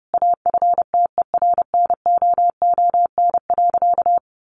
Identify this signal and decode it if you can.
Morse code, heard as a keyed tone, about 30 wpm, 700 Hz. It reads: AFTERNOON.
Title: Sentences from Top 1500 Words 30wpm